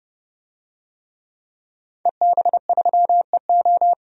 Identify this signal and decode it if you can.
Morse code, heard as a keyed tone, about 28 wpm, 690 Hz. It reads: EB3EO
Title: DXCC-100 Call Signs - 5 Characters - 30wpm